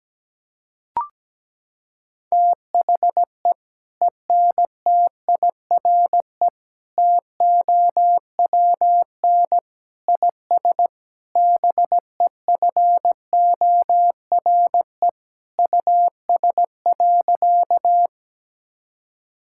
Morse code, heard as a keyed tone, about 17 wpm, 700 Hz. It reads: THE ENTIRE TOWN IS BEFORE US.
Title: Sentences from Top 1000 Words - Review Part 5 - 17wpm